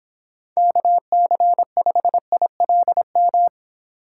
Morse code, heard as a keyed tone, about 25 wpm, 700 Hz. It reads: KC5ILM
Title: Speed Racing - USA 2x3 Call Signs 22wpm